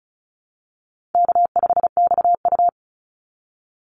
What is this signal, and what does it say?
Morse code, heard as a keyed tone, about 34 wpm, 700 Hz. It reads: K5XU
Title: Easy Call Signs Repeated 2x - 35wpm